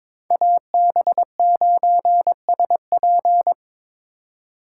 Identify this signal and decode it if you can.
Morse code, heard as a keyed tone, about 25 wpm, 700 Hz. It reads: AB9SP
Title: Speed Racing - USA 2x2 Call Signs 22wpm